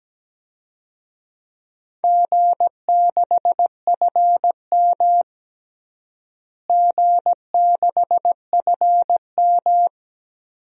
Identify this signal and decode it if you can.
Morse code, heard as a keyed tone, about 17 wpm, 700 Hz. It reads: G6FM G6FM
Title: DXCC-100 Call Signs - 4 Characters - Repeated 2x - 17wpm